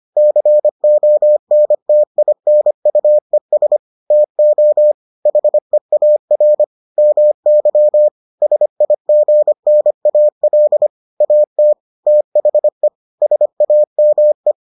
Morse code, heard as a keyed tone, about 25 wpm, 600 Hz. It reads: CONTINUES TO HEAR MY SIGNAL AT THE SAME